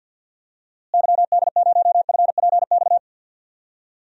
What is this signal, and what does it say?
Morse code, heard as a keyed tone, about 52 wpm, 700 Hz. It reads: YD0FPX
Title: DXCC-100 Call Signs - 6 Characters - Repeated 3x - 50wpm